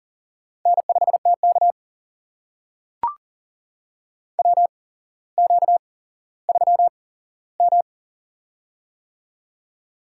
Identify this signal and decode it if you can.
Morse code, heard as a keyed tone, about 40 wpm, spaced as for 17 wpm, 700 Hz. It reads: N5TK WQ3M
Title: USA Call Signs - 4 Characters - 8x Character Spacing - 40wpm